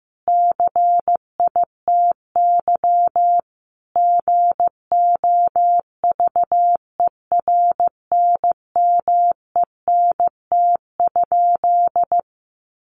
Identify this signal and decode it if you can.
Morse code, heard as a keyed tone, about 15 wpm, 700 Hz. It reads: CITY GOVERNMENT?
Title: Sentences from Top 1400 Words 15wpm